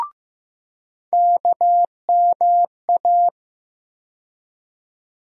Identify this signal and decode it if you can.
Morse code, heard as a keyed tone, about 15 wpm, 700 Hz. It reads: KMA
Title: USA Call Sign Postfixes - 3 Letters 15wpm